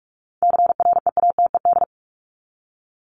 Morse code, heard as a keyed tone, about 44 wpm, 700 Hz. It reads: CREATED
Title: ED Suffix - Top 100 Words with Base Words 45wpm